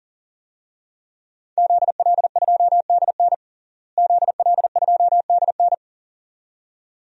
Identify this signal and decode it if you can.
Morse code, heard as a keyed tone, about 40 wpm, 700 Hz. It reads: ZL2DN ZL2DN
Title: DXCC-100 Call Signs - 5 Characters - Repeated 2x - 40wpm